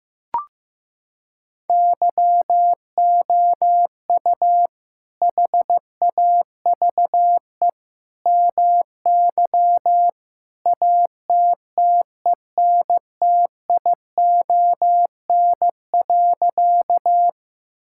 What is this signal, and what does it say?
Morse code, heard as a keyed tone, about 16 wpm, 700 Hz. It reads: YOU HAVE MY ATTENTION.
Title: Sentences from Top 1000 Words - Review Part 6 - 15wpm